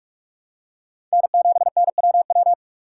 Morse code, heard as a keyed tone, about 45 wpm, 700 Hz. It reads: N7NWW
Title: USA Call Signs - 5 Characters - 2x Character Spacing - 45wpm